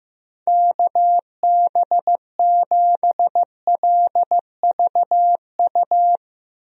Morse code, heard as a keyed tone, about 15 wpm, 700 Hz. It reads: KB7LVU
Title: USA Call Signs - 6 Characters - Repeated 3x - 15wpm